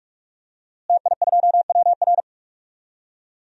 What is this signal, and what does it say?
Morse code, heard as a keyed tone, about 46 wpm, 700 Hz. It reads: TI2WR
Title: DXCC-100 Call Signs - 5 Characters - Repeated 3x - 45wpm